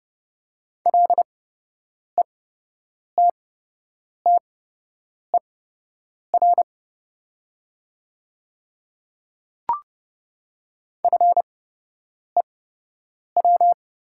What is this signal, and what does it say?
Morse code, heard as a keyed tone, about 30 wpm, spaced as for 8 wpm, 700 Hz. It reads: LETTER FEW
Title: 8x Character Spacing - Top 300 Words - 30wpm